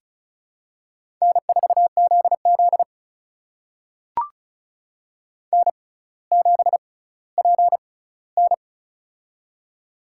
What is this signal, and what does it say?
Morse code, heard as a keyed tone, about 36 wpm, spaced as for 18 wpm, 700 Hz. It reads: N4ZZ N7PN
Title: USA Call Signs - 4 Characters - 6x Character Spacing - 35wpm